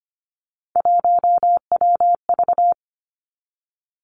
Morse code, heard as a keyed tone, about 25 wpm, 700 Hz. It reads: 1WV
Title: Three Challenging Letter-Number Combinations 25wpm